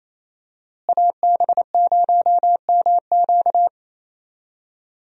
Morse code, heard as a keyed tone, about 28 wpm, 700 Hz. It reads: AB0MQ
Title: USA Call Signs - 5 Characters - Repeated 2x - 28wpm